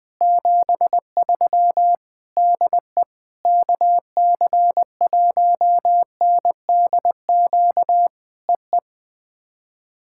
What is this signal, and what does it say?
Morse code, heard as a keyed tone, about 20 wpm, 700 Hz. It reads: 73 DE KC1NDQ EE
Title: POTA QSO Practice - 20wpm